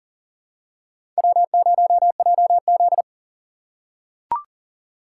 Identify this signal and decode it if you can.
Morse code, heard as a keyed tone, about 40 wpm, 700 Hz. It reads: W0JZ
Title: USA Call Signs - 6x Character Spacing - 40wpm